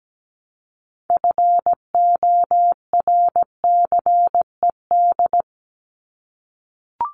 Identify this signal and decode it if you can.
Morse code, heard as a keyed tone, about 17 wpm, 700 Hz. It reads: FORCED
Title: ED Suffix - Top 100 Words with Base Words 17wpm